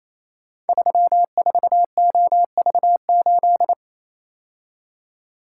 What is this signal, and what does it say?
Morse code, heard as a keyed tone, about 27 wpm, 700 Hz. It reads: 34OV8
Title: Code Groups - 5 Letter:Numbers - 28wpm